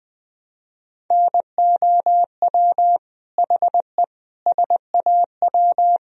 Morse code, heard as a keyed tone, about 21 wpm, 700 Hz. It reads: NOW HE SAW
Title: Sentences from Top 1000 Words - Review Part 4 - 20wpm